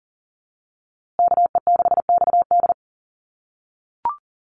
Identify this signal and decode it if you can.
Morse code, heard as a keyed tone, about 40 wpm, 700 Hz. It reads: KE6XD
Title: USA Call Signs Repeated 3x 40wpm